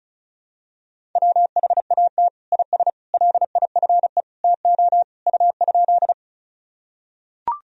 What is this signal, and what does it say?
Morse code, heard as a keyed tone, about 34 wpm, 700 Hz. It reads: WHAT IS LIFE TO U?
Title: Sentences from Top 300 Words and Abbreviations 35wpm